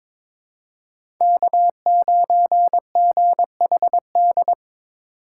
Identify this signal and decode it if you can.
Morse code, heard as a keyed tone, about 22 wpm, 700 Hz. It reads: K9GHD
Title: USA Call Signs - 4x Character Spacing - 22wpm